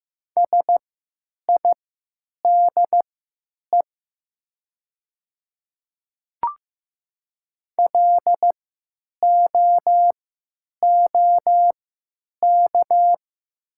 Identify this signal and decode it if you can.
Morse code, heard as a keyed tone, about 15 wpm, spaced as for 9 wpm, 700 Hz. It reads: SIDE LOOK
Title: 3x Character Spacing - Top 100 Words - 15wpm